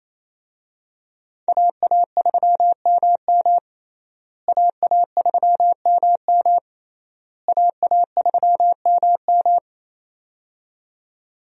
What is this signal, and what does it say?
Morse code, heard as a keyed tone, about 28 wpm, 700 Hz. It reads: AA3MM AA3MM AA3MM
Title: USA Call Signs - 5 Characters - Repeated 3x - 28wpm